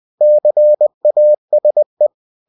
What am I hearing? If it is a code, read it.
Morse code, heard as a keyed tone, about 20 wpm, 600 Hz. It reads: CASE
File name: cw-books / winnie-the-pooh / winnie-20wpm-page-0023